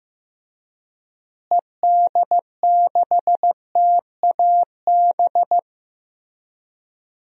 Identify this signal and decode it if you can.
Morse code, heard as a keyed tone, about 15 wpm, 700 Hz. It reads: ED6TAB
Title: DXCC-100 Call Signs - 6 Characters - Repeated 3x - 15wpm